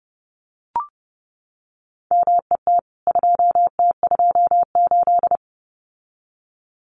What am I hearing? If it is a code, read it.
Morse code, heard as a keyed tone, about 30 wpm, 700 Hz. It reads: MET 2T28
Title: CWT Exchange 30wpm